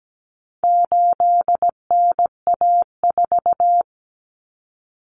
Morse code, heard as a keyed tone, about 17 wpm, 700 Hz. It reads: 8NA4
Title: Common Cut Numbers - 4-Digit 17wpm